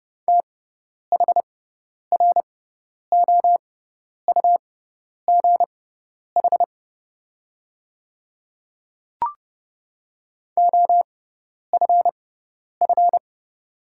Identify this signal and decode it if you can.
Morse code, heard as a keyed tone, about 32 wpm, spaced as for 10 wpm, 700 Hz. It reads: THROUGH OFF
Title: 6x Character Spacing - Top 200 Words - 30wpm